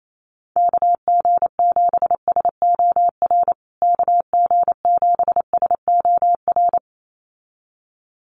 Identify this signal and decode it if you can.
Morse code, heard as a keyed tone, about 28 wpm, 700 Hz. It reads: KG7SOR KG7SOR
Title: USA Call Signs Repeated 2x 28wpm